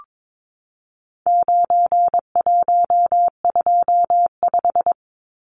Morse code, heard as a keyed tone, about 22 wpm, 700 Hz. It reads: 9125
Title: Numbers - 4-Digit 22wpm